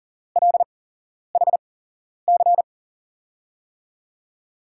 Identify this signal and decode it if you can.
Morse code, heard as a keyed tone, about 39 wpm, spaced as for 11 wpm, 700 Hz. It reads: LHC
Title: USA Call Signs - 5 Characters - 8x Character Spacing - 40wpm